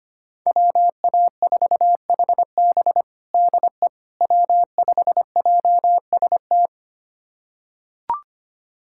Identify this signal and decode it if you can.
Morse code, heard as a keyed tone, about 25 wpm, 700 Hz. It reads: WA4HB DE W5JST
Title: USA Callsign DE Callsign - 5 Characters - 25wpm